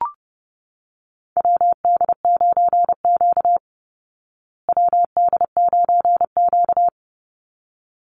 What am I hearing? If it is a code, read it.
Morse code, heard as a keyed tone, about 30 wpm, 700 Hz. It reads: WD9Q WD9Q
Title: Call Signs Repeated 3x - 30wpm